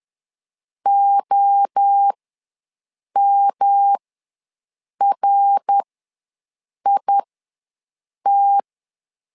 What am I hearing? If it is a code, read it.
Morse code, heard as a keyed tone, about 10 wpm, spaced as for 6 wpm, 800 Hz. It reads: OMRIT
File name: LEZIONI / 3 MWRO / 3 MWRO ws 10wpm 2310